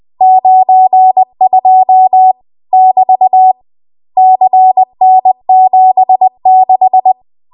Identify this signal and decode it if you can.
Morse code, heard as a keyed tone, about 20 wpm, 750 Hz. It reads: 92 <BT> CN76
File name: Morse / 20WPM